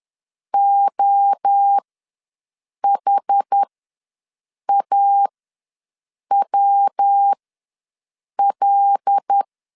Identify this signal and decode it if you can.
Morse code, heard as a keyed tone, about 10 wpm, spaced as for 6 wpm, 800 Hz. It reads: OHAWL
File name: LEZIONI / 5 LBVC / 5 LBVC ws 10wpm 3733